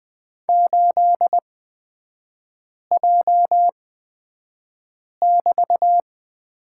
Morse code, heard as a keyed tone, about 20 wpm, spaced as for 5 wpm, 700 Hz. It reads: 8J<BT>
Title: 224 Lesson 36e - BT - Three Characters 20wpm